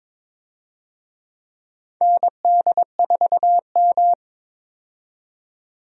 Morse code, heard as a keyed tone, about 22 wpm, 700 Hz. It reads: ND4M
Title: USA Call Signs - 4 Characters - 22wpm